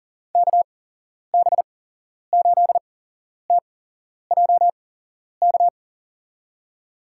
Morse code, heard as a keyed tone, about 40 wpm, spaced as for 11 wpm, 700 Hz. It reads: KB8TJK
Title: USA Call Signs - 6 Characters - 8x Character Spacing - 40wpm